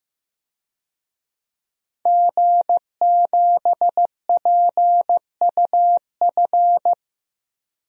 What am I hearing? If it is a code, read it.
Morse code, heard as a keyed tone, about 15 wpm, 700 Hz. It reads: G7PUF
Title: DXCC-100 Call Signs - 5 Characters - 15wpm